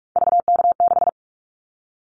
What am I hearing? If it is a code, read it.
Morse code, heard as a keyed tone, about 46 wpm, 700 Hz. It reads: VK6
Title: Callsign Prefixes for Top 50 DXCC - 45wpm